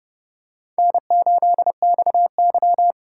Morse code, heard as a keyed tone, about 30 wpm, 700 Hz. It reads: N8XY
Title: USA Call Signs - 4 Characters - 4x Character Spacing - 30wpm